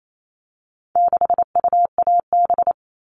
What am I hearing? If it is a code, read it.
Morse code, heard as a keyed tone, about 28 wpm, 700 Hz. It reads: 6UAB
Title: All Cut Numbers - 4-Digit 28wpm